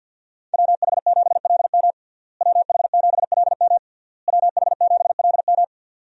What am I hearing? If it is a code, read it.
Morse code, heard as a keyed tone, about 52 wpm, 700 Hz. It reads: WH7LM WH7LM WH7LM
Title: USA Call Signs - 5 Characters - Repeated 3x - 50wpm